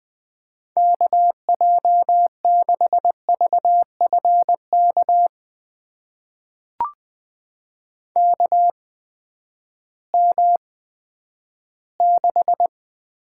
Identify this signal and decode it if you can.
Morse code, heard as a keyed tone, about 21 wpm, spaced as for 11 wpm, 700 Hz. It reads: KJ6VFK KM6
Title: USA Call Signs - 6 Characters - 8x Character Spacing - 20wpm